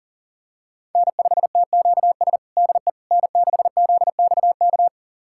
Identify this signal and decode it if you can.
Morse code, heard as a keyed tone, about 40 wpm, 700 Hz. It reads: N5TQS DE N6ZXK
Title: USA Callsign DE Callsign - 5 Characters - 40wpm